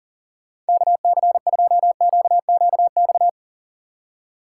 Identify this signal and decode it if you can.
Morse code, heard as a keyed tone, about 38 wpm, 700 Hz. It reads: KC2QQX
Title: Speed Racing - USA 2x3 Call Signs 40wpm